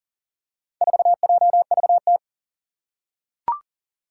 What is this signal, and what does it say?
Morse code, heard as a keyed tone, about 40 wpm, 700 Hz. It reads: 4JVT
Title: Four Challenging Letter-Number Combinations 40wpm